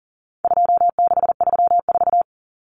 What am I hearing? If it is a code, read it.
Morse code, heard as a keyed tone, about 40 wpm, 700 Hz. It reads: 2634
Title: Numbers - 4-Digit 40wpm